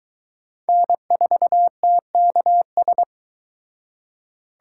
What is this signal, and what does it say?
Morse code, heard as a keyed tone, about 22 wpm, 700 Hz. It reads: N4TKS